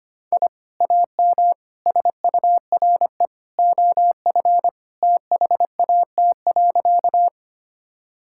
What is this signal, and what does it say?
Morse code, heard as a keyed tone, about 25 wpm, 700 Hz. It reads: I AM SURE OF THAT.
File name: practice / 25wpm / Sentences with 2-Word Phrases - 25wpm